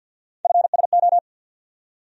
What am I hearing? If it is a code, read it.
Morse code, heard as a keyed tone, about 51 wpm, 700 Hz. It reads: FSO
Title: USA Call Sign Postfixes - 3 Letters 50wpm